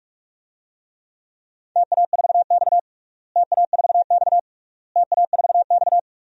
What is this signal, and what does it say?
Morse code, heard as a keyed tone, about 46 wpm, 700 Hz. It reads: TA4X TA4X TA4X
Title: DXCC-100 Call Signs - 4 Characters - Repeated 3x - 45wpm